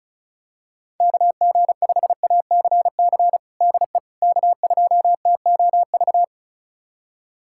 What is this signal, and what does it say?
Morse code, heard as a keyed tone, about 34 wpm, 700 Hz. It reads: KG5ACC DE K2TOV
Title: USA Callsign DE Callsign 35wpm